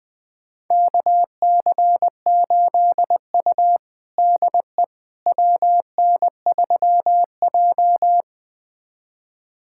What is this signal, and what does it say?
Morse code, heard as a keyed tone, about 20 wpm, 700 Hz. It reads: KC8U DE WN3J
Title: USA Callsign DE Callsign - 4 Characters - 20wpm